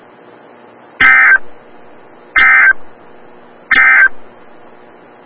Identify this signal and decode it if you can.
Emergency Alert System (SAME) tones, heard as harsh buzzing bursts. It is heard from 0.9 to 4.2 s.
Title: Des Moines, IA NOAA Weather Radio - WXL57